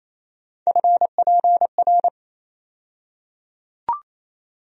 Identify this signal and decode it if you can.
Morse code, heard as a keyed tone, about 28 wpm, 700 Hz. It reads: FPR
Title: USA Call Sign Postfixes - 3 Letters 28wpm